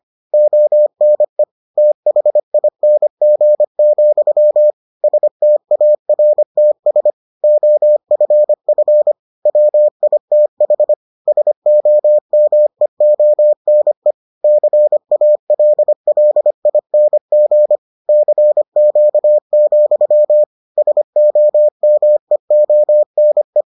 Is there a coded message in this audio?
Yes: Morse code from 0.3 to 23.6 s, heard as a keyed tone, about 25 wpm, 600 Hz. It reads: ONE THING, STARTS OFF WITH SOMEONE CALLING CQ, SOMEONE